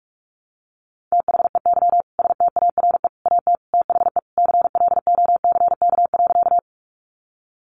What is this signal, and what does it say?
Morse code, heard as a keyed tone, about 44 wpm, 700 Hz. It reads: THEY STARE AT THE CLOCK.
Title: Easy Phrases 45wpm